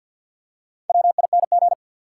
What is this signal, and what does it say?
Morse code, heard as a keyed tone, about 53 wpm, 700 Hz. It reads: WING
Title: Four Letter Words - 50wpm